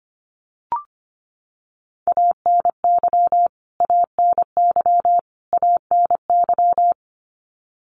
Morse code, heard as a keyed tone, about 25 wpm, 700 Hz. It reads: ANY ANY ANY
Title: Top 200 Words Repeated 25wpm